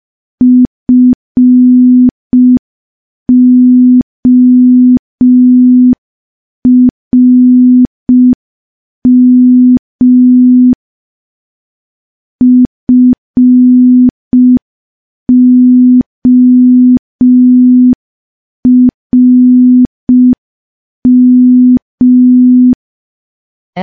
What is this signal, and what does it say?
Morse code, heard as a keyed tone, about 5 wpm, 250 Hz. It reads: FORM FORM
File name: morse / morse-training-05WPM